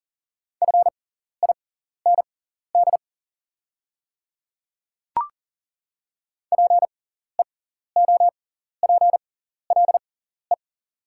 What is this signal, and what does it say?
Morse code, heard as a keyed tone, about 38 wpm, spaced as for 14 wpm, 690 Hz. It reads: FIND PEOPLE
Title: 6x Character Spacing - Top 100 Words - 40wpm